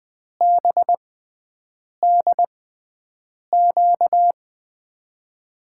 Morse code, heard as a keyed tone, about 20 wpm, spaced as for 7 wpm, 700 Hz. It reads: BDQ
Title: USA Call Signs - 6 Characters - 6x Character Spacing - 20wpm